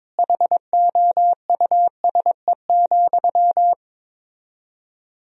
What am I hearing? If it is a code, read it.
Morse code, heard as a keyed tone, about 22 wpm, 690 Hz. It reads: HOUSE,
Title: 4x Word-Spacing - Sentences from Top 400 Words - 22wpm